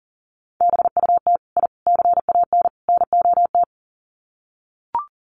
Morse code, heard as a keyed tone, about 40 wpm, 700 Hz. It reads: BUT I CAN NOT
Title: Sets of 4 Words 40wpm